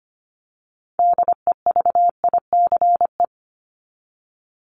Morse code, heard as a keyed tone, about 25 wpm, 700 Hz. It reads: DEVICE
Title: Top 1251-1500 Words 25wpm